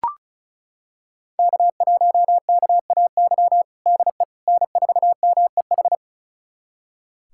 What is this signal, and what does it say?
Morse code, heard as a keyed tone, about 36 wpm, 700 Hz. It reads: K1KAY DE N4MEH
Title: USA Callsign DE Callsign - 5 Characters - 35wpm